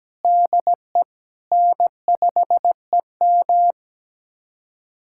Morse code, heard as a keyed tone, about 17 wpm, 700 Hz. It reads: DE N5EM
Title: USA Callsign DE Callsign - 4 Characters - 17wpm